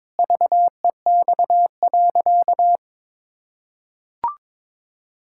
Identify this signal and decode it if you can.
Morse code, heard as a keyed tone, about 22 wpm, 700 Hz. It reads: VEX.
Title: Pangram Challenge v2 - 22wpm